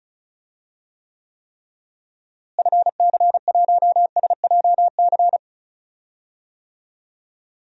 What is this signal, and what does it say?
Morse code, heard as a keyed tone, about 36 wpm, 700 Hz. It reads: FC1SJC
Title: DXCC-100 Call Signs - 6 Characters - Repeated 3x - 35wpm